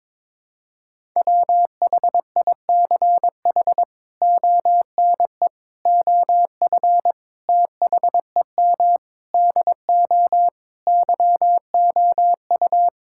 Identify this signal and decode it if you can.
Morse code, heard as a keyed tone, about 22 wpm, 700 Hz. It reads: WHICH ONE OF THEM DO YOU
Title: Sentences from Top 1000 Words - Review Part 5 - 22wpm